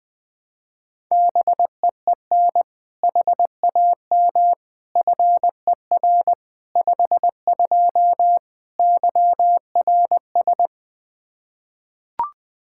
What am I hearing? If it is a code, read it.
Morse code, heard as a keyed tone, about 21 wpm, 700 Hz. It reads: BEEN HAM FER 52 YRS
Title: Practice QSO - Exchange 3 - More Info - 20wpm